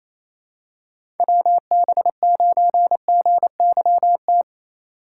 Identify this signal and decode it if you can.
Morse code, heard as a keyed tone, about 28 wpm, 700 Hz. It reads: WB9GYT
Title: USA Call Signs - 6 Characters - 2x Character Spacing - 28wpm